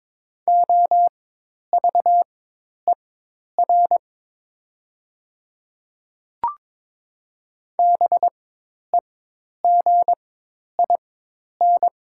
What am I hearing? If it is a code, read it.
Morse code, heard as a keyed tone, about 22 wpm, spaced as for 10 wpm, 700 Hz. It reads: OVER BEGIN
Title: 4x Character Spacing - Top 300 Words - 22wpm